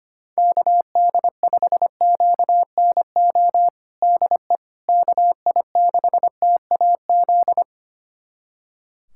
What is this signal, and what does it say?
Morse code, heard as a keyed tone, about 25 wpm, 700 Hz. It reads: KD5QNO DE KI6TAZ
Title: USA Callsign DE Callsign 25wpm